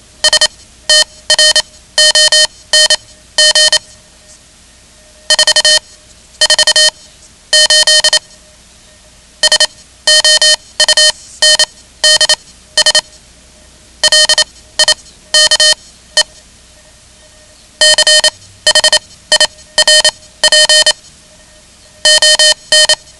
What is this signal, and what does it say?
Morse code, heard as a keyed tone, about 27 wpm, spaced as for 16 wpm, 3050 Hz. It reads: STRONG 4 4 8 SOUNDS LIKE CHIRP ON
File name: ham radio / CodeQuick / Code Quick 10